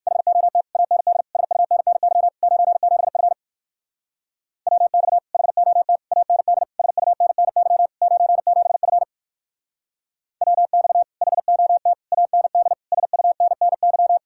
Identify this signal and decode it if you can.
Morse code, heard as a keyed tone, about 60 wpm, 700 Hz. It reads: HOT AND SUNNY 97F WX HOT AND SUNNY 97F WX HOT AND SUNNY